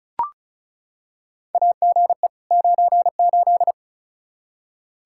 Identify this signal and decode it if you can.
Morse code, heard as a keyed tone, about 36 wpm, 700 Hz. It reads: AGE 98
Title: Practice QSO - Exchange 3 - More Info - 35wpm